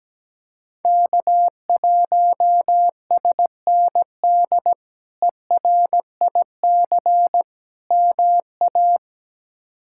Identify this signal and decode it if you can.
Morse code, heard as a keyed tone, about 17 wpm, 700 Hz. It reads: K1SND ERIC MA
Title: K1USN SST QSO Practice - 17wpm